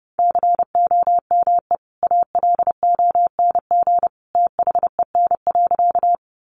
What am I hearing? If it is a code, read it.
Morse code, heard as a keyed tone, about 30 wpm, 700 Hz. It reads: COME ALONG THEN.
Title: Sentences from Top 700 Words 30wpm